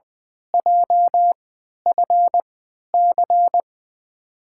Morse code, heard as a keyed tone, about 20 wpm, spaced as for 11 wpm, 700 Hz. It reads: JFC
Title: USA Call Signs - 5 Characters - 3x Character Spacing - 20wpm